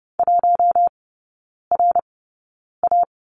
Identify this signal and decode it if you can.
Morse code, heard as a keyed tone, about 30 wpm, 700 Hz. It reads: 1 R A
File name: 081 Lesson 14e - Number 2 - Three Characters 30wpm